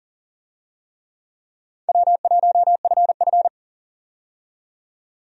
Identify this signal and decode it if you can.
Morse code, heard as a keyed tone, about 39 wpm, 700 Hz. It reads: W1FF
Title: USA Call Signs - 4 Characters - 40wpm